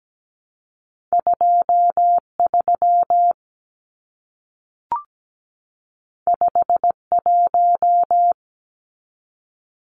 Morse code, heard as a keyed tone, about 17 wpm, 700 Hz. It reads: 23 51
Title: Numbers - 2-Digit 17wpm